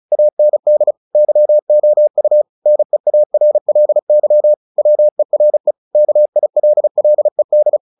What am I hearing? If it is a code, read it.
Morse code, heard as a keyed tone, about 34 wpm, 600 Hz. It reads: AND YOU NEARLY WERE KILLED
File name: the old man and the sea / the old man and the sea 1 35wpm